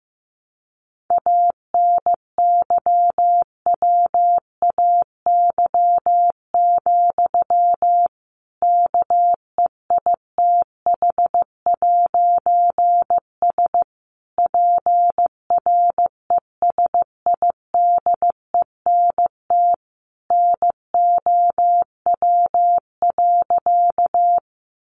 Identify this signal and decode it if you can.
Morse code, heard as a keyed tone, about 15 wpm, 700 Hz. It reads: ANYWAY, KEITH'S PRESIDENT NOW.
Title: A Sound Of Thunder 15wpm